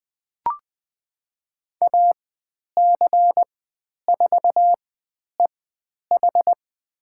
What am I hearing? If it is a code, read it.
Morse code, heard as a keyed tone, about 20 wpm, spaced as for 10 wpm, 700 Hz. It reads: AC4EH
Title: 102 Lesson 17g - Letter u - Call Signs 20wpm